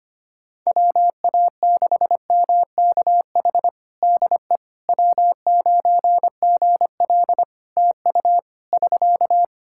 Morse code, heard as a keyed tone, about 25 wpm, 700 Hz. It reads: WA6MKH DE W9GL TU <SK>
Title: Practice QSO - Exchange 4 - Ending - 25wpm